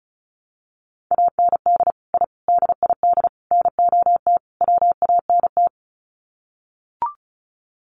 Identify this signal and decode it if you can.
Morse code, heard as a keyed tone, about 34 wpm, 700 Hz. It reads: AND I DID NOT WANT
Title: Sets of 5 Words 35wpm